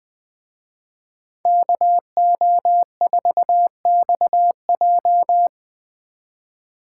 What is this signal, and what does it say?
Morse code, heard as a keyed tone, about 22 wpm, 700 Hz. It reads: KO4XJ
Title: Speed Racing - USA 2x2 Call Signs 20wpm